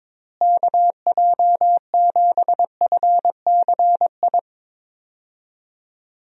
Morse code, heard as a keyed tone, about 22 wpm, 700 Hz. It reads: KJ7FCI
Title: USA Call Signs - 6 Characters - Repeated 3x - 22wpm